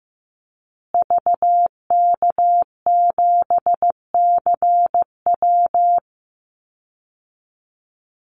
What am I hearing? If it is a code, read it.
Morse code, heard as a keyed tone, about 15 wpm, 700 Hz. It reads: VK7CW
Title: Call Signs Repeated 2x - 15wpm